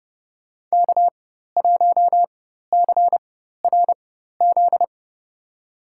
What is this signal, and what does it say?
Morse code, heard as a keyed tone, about 30 wpm, spaced as for 14 wpm, 700 Hz. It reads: K1CRZ
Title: USA Call Signs - 5 Characters - 4x Character Spacing - 30wpm